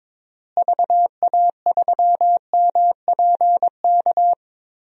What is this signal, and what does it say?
Morse code, heard as a keyed tone, about 22 wpm, 700 Hz. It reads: VA3MPK
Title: DXCC-100 Call Signs - 6 Characters - Repeated 2x - 22wpm